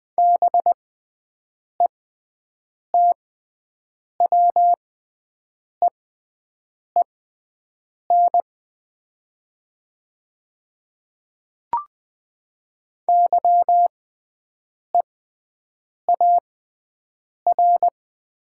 Morse code, heard as a keyed tone, about 20 wpm, spaced as for 7 wpm, 700 Hz. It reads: BETWEEN YEAR
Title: 6x Character Spacing - Top 300 Words - 20wpm